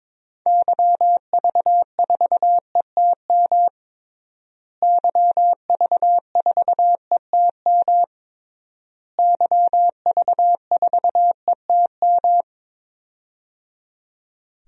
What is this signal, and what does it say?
Morse code, heard as a keyed tone, about 22 wpm, 700 Hz. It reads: YV4ETM YV4ETM YV4ETM
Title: DXCC-100 Call Signs - 6 Characters - Repeated 3x - 22wpm